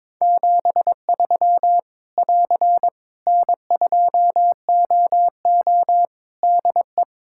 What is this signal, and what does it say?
Morse code, heard as a keyed tone, about 22 wpm, 700 Hz. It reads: 73 <AR> N2OO DE
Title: Practice QSO - Exchange 4 - Ending - 22wpm